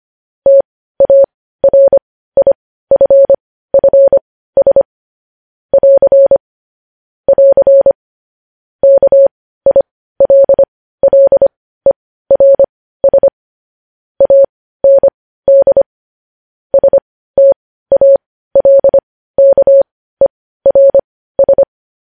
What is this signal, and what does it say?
Morse code, heard as a keyed tone, about 25 wpm, spaced as for 15 wpm, 550 Hz. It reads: TARIFFS <AR> <AR> KILLERS AND STALKERS
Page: news headlines in morse code at 15 wpm